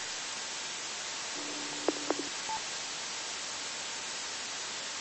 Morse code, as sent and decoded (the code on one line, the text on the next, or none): none